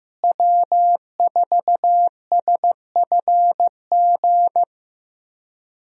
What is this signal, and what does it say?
Morse code, heard as a keyed tone, about 15 wpm, 700 Hz. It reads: W4SFG
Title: USA Call Signs - 5 Characters - Repeated 2x - 15wpm